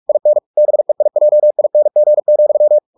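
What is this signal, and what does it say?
Morse code, heard as a keyed tone, about 44 wpm, 600 Hz. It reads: IN BEIJING,
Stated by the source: Morse practice at 45 wpm, the novel Never by Ken Follett in English, from CW med Gnister.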